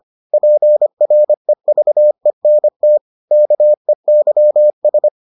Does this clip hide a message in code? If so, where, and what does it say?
Yes: Morse code from 0.3 to 5.1 s, heard as a keyed tone, about 25 wpm, 600 Hz. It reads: PREVENT KEYS